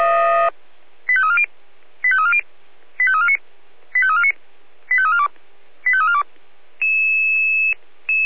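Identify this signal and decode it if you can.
Five-tone selective call (ZVEI), heard as quick runs of five tones, tones of about 70 ms. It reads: 85340 85340 85349 85349 85342 85342
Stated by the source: ZVEI five-tone selective call received by SDR.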